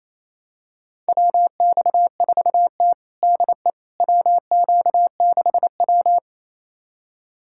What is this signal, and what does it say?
Morse code, heard as a keyed tone, about 28 wpm, 700 Hz. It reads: WX4T DE WQ6W
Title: USA Callsign DE Callsign - 4 Characters - 28wpm